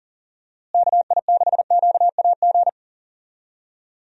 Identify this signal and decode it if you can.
Morse code, heard as a keyed tone, about 40 wpm, 700 Hz. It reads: KI6QAG
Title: USA Call Signs - 6 Characters - Repeated 3x - 40wpm